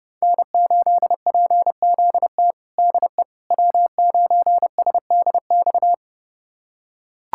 Morse code, heard as a keyed tone, about 30 wpm, 700 Hz. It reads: N8PZT DE W9SDX
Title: USA Callsign DE Callsign - 5 Characters - 30wpm